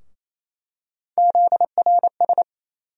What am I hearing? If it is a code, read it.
Morse code, heard as a keyed tone, about 28 wpm, 700 Hz. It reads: ZRS
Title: USA Call Sign Postfixes - 3 Letters 28wpm